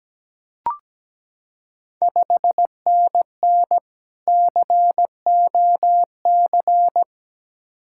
Morse code, heard as a keyed tone, about 17 wpm, 700 Hz. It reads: 5NN COC
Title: Ontario QSO Party Exchange 17wpm